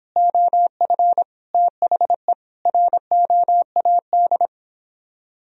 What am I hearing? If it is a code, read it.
Morse code, heard as a keyed tone, about 29 wpm, 700 Hz. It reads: OF THE ROAD